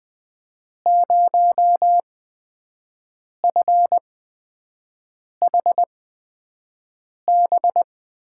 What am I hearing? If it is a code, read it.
Morse code, heard as a keyed tone, about 20 wpm, spaced as for 5 wpm, 700 Hz. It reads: 0FHB
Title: USA Call Signs - 5 Characters - 8x Character Spacing - 20wpm